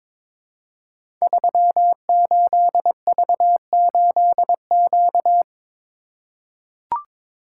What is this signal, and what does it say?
Morse code, heard as a keyed tone, about 23 wpm, 700 Hz. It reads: 38V8Q
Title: Code Groups - 5 Letter:Numbers - 22wpm